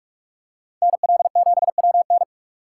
Morse code, heard as a keyed tone, about 46 wpm, 700 Hz. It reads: NL7WN
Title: USA Call Signs - 3x Character Spacing - 45wpm